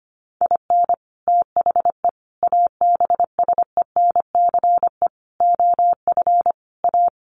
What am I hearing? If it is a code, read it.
Morse code, heard as a keyed tone, about 25 wpm, 700 Hz. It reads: IN THE ABSENCE OF A
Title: Sets of 5 Words Encore 25wpm